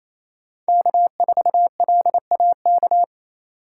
Morse code, heard as a keyed tone, about 28 wpm, 700 Hz. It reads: K4LAK
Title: USA Call Signs - 8x Character Spacing - 28wpm